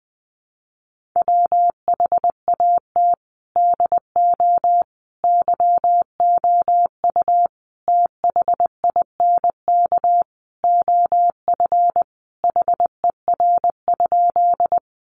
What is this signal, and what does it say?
Morse code, heard as a keyed tone, about 20 wpm, 700 Hz. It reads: WHAT DO YOU THINK OF HER?
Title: Sentences from Top 200 Words 20wpm